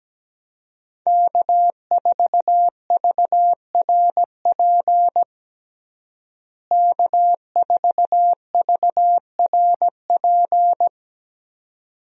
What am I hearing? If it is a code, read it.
Morse code, heard as a keyed tone, about 17 wpm, 700 Hz. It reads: K4VRP K4VRP
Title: USA Call Signs - 5 Characters - Repeated 2x - 17wpm